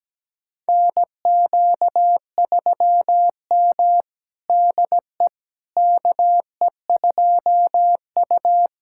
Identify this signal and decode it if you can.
Morse code, heard as a keyed tone, about 17 wpm, 700 Hz. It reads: NQ3M DE KE2U
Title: USA Callsign DE Callsign - 4 Characters - 17wpm